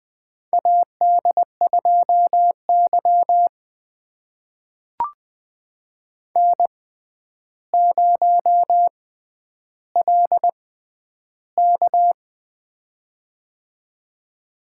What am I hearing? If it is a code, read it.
Morse code, heard as a keyed tone, about 20 wpm, spaced as for 10 wpm, 700 Hz. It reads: AD2Y N0LK
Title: USA Call Signs - 4 Characters - 6x Character Spacing - 20wpm